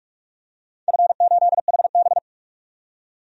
Morse code, heard as a keyed tone, about 46 wpm, 700 Hz. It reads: F8HB
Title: DXCC-100 Call Signs - 4 Characters - 45wpm